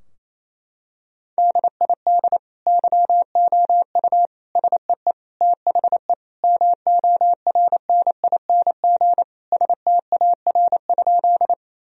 Morse code, heard as a keyed tone, about 28 wpm, 700 Hz. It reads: DID YOU SEE THE MORNING STAR?
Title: Sentences from Top 1000 Words - Review Part 2 - 28wpm